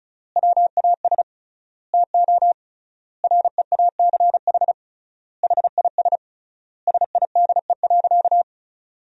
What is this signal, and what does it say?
Morse code, heard as a keyed tone, about 36 wpm, 700 Hz. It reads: WAS TO REACH HIS SIDE.
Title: 3x Word-Spacing - Sentences from Top 400 Words - 35wpm